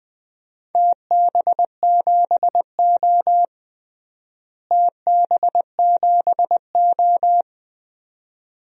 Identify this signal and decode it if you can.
Morse code, heard as a keyed tone, about 20 wpm, 700 Hz. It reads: TB7O TB7O
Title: DXCC-100 Call Signs - 4 Characters - Repeated 2x - 20wpm